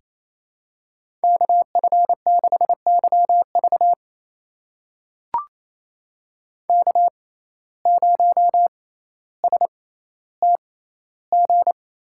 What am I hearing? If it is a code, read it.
Morse code, heard as a keyed tone, about 27 wpm, spaced as for 14 wpm, 700 Hz. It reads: KF6YV K0STG
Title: USA Call Signs - 5 Characters - 6x Character Spacing - 28wpm